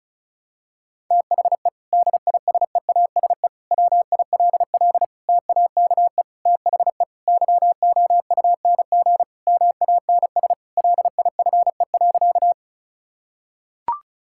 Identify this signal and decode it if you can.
Morse code, heard as a keyed tone, about 36 wpm, 700 Hz. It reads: THE DISEASE WILL TAKE THE YOUNG MANS LIFE.
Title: Sentences from Top 1000 Words - Review Part 6 - 35wpm